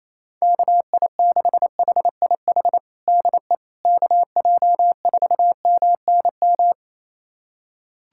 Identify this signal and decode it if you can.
Morse code, heard as a keyed tone, about 28 wpm, 700 Hz. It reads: KI6HIH DE KJ4MNM
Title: USA Callsign DE Callsign 28wpm